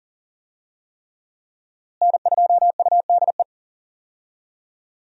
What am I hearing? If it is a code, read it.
Morse code, heard as a keyed tone, about 40 wpm, 700 Hz. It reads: N2UDE
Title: USA Call Signs - 5 Characters - Repeated 3x - 40wpm